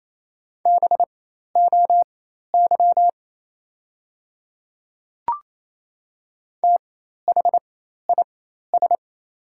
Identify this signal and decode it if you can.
Morse code, heard as a keyed tone, about 28 wpm, spaced as for 13 wpm, 700 Hz. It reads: BOY THIS
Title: 4x Character Spacing - Top 200 Words - 28wpm